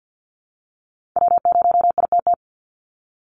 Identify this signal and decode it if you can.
Morse code, heard as a keyed tone, about 50 wpm, 700 Hz. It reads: W0ITT
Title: Call Signs Repeated 2x - 50wpm